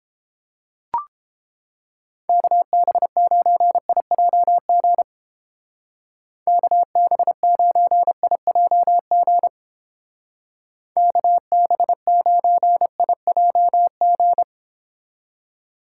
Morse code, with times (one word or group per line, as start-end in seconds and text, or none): 2.3-5.0 KB9IJG
6.5-9.5 KB9IJG
11.0-14.4 KB9IJG